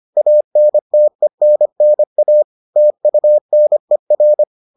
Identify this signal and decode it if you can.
Morse code, heard as a keyed tone, about 25 wpm, 600 Hz. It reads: ANTENNA TUNER